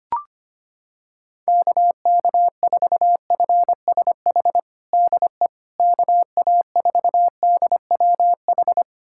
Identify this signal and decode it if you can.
Morse code, heard as a keyed tone, about 25 wpm, 700 Hz. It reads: KK4FSH DE KA4DWH
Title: USA Callsign DE Callsign - 6 Characters - 25wpm